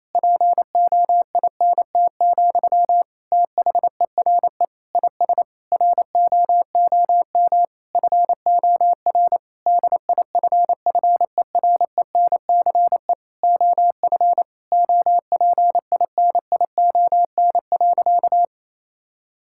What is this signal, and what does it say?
Morse code, heard as a keyed tone, about 28 wpm, 700 Hz. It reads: POINT, THERE IS ROOM FOR DIFFERENCE OF OPINION.
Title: Sentences from Top 1600 Words 28wpm